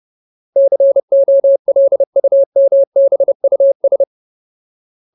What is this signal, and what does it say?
Morse code, heard as a keyed tone, about 30 wpm, 550 Hz. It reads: COLUMBUS
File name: Ultimate Word List / 30WPM Ultimate